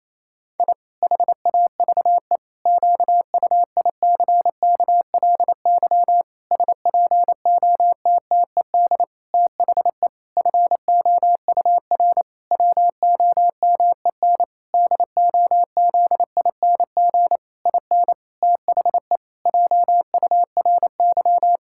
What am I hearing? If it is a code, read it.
Morse code, heard as a keyed tone, about 28 wpm, 700 Hz. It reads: I HAVE QUICKLY SPOTTED THE FOUR WOMEN DOZING IN THE JURY
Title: Pangram Challenge v2 - 28wpm